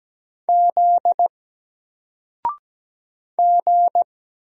Morse code, heard as a keyed tone, about 17 wpm, 700 Hz. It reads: Z G
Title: Single Character (Letter:Number:Punctuation:Prosign) 17wpm